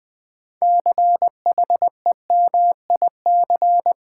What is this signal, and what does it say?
Morse code, heard as a keyed tone, about 20 wpm, 700 Hz. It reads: CHEMIC
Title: Build A Word - 2 Letters At A Time - 8-Letter Words - 20wpm